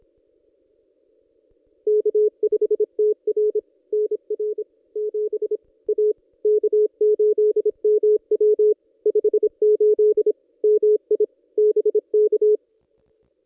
Morse code, heard as a keyed tone, about 26 wpm, 420 Hz. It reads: K5TR NR 7 A K8MW 58 MI BK
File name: contests / ss-cw-k5tr-op-n5rz -1600